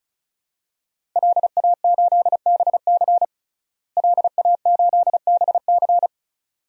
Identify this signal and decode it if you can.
Morse code, heard as a keyed tone, about 36 wpm, 700 Hz. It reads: LA8BC LA8BC
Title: DXCC-100 Call Signs - 5 Characters - Repeated 2x - 35wpm